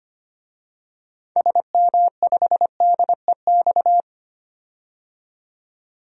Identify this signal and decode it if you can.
Morse code, heard as a keyed tone, about 25 wpm, 700 Hz. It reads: SM5DEX
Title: DXCC-100 Call Signs - 6 Characters - 25wpm